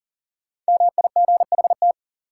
Morse code, heard as a keyed tone, about 40 wpm, 700 Hz. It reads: MIGHT
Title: Sets of 2 Words - New Word Drill - 40wpm